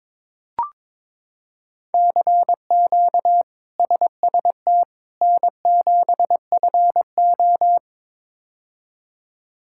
Morse code, heard as a keyed tone, about 22 wpm, 700 Hz. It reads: CQ SST N7FO
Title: K1USN SST QSO Practice - 22wpm